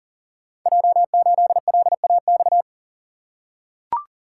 Morse code, heard as a keyed tone, about 40 wpm, 690 Hz. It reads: J8LAX
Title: Code Groups - 5 Letter:Numbers - 40wpm